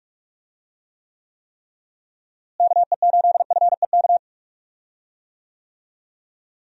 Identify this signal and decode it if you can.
Morse code, heard as a keyed tone, about 42 wpm, 700 Hz. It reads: KE8FEK
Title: USA Call Signs - 6 Characters - Repeated 3x - 45wpm